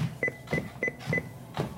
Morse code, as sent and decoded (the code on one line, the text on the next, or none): none